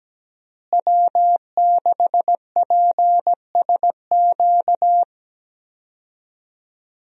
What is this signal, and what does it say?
Morse code, heard as a keyed tone, about 17 wpm, 700 Hz. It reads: W6PSQ
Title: USA Call Signs - 5 Characters - Repeated 3x - 17wpm